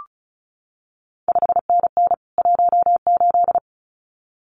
Morse code, heard as a keyed tone, about 36 wpm, 700 Hz. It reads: 5NN 18
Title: CQ WW DX Contest Exchange Encore - 35wpm